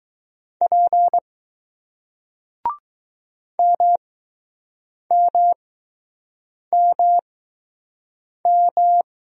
Morse code, heard as a keyed tone, about 20 wpm, spaced as for 6 wpm, 700 Hz. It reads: P MMMM